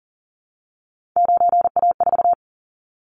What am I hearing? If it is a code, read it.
Morse code, heard as a keyed tone, about 38 wpm, 700 Hz. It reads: 9A4
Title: Callsign Prefixes for Top 50 DXCC - 40wpm